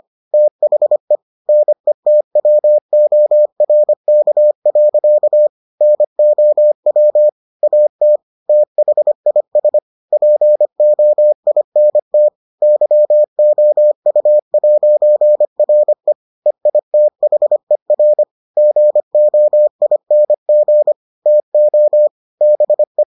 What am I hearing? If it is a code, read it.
Morse code, heard as a keyed tone, about 25 wpm, 600 Hz. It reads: THE NETWORK. NOW AT THIS POINT YOU'RE EITHER GOING TO BE